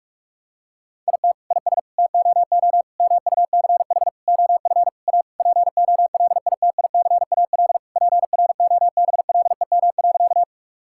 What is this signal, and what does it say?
Morse code, heard as a keyed tone, about 41 wpm, 700 Hz. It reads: IT IS TOO MUCH OF A POLITICAL PROBLEM.
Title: Sentences from Top 1000 Words - Review Part 5 - 45wpm